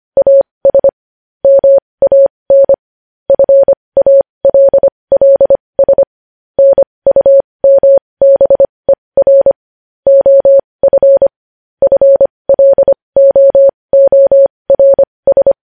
Morse code, heard as a keyed tone, about 25 wpm, spaced as for 20 wpm, 550 Hz. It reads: AS MAN FALLS NUMBER OF FLOORS
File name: podcast / News headlines-1x-25x20